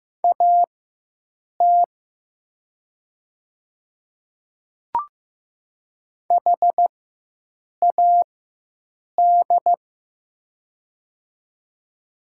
Morse code, heard as a keyed tone, about 15 wpm, spaced as for 7 wpm, 700 Hz. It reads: AT HAD
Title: 4x Character Spacing - Top 100 Words - 15wpm